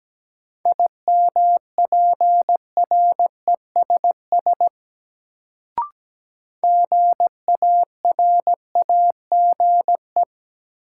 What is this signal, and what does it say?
Morse code, heard as a keyed tone, about 17 wpm, 700 Hz. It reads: IMPRESS GARAGE
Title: Sentences from Top 1600 Words - New Word Drill - 17wpm